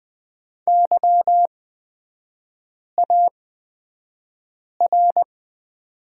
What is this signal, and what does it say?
Morse code, heard as a keyed tone, about 20 wpm, spaced as for 5 wpm, 700 Hz. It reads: YAR
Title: 236 Lesson 38e - Letter q - Three Characters 20wpm